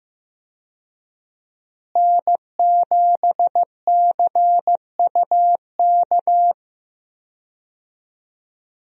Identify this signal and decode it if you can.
Morse code, heard as a keyed tone, about 15 wpm, 700 Hz. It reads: N7CUK
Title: USA Call Signs - 5 Characters - Repeated 2x - 15wpm